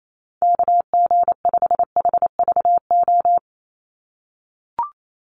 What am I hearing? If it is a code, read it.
Morse code, heard as a keyed tone, about 28 wpm, 700 Hz. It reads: KG5HVO
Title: Call Signs - 28wpm